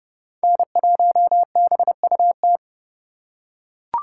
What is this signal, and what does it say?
Morse code, heard as a keyed tone, about 30 wpm, 700 Hz. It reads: N1BUT
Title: USA Call Signs - 5 Characters - 3x Character Spacing - 30wpm